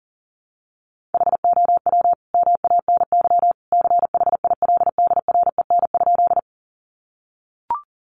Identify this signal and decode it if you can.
Morse code, heard as a keyed tone, about 40 wpm, 700 Hz. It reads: HOW MANY CHILDREN?
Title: Sentences from Top 300 Words 40wpm